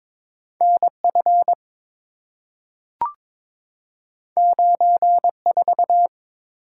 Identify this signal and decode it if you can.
Morse code, heard as a keyed tone, about 22 wpm, 700 Hz. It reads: NF 94
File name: practice / 22wpm / Two Letter-Number Combinations - 22wpm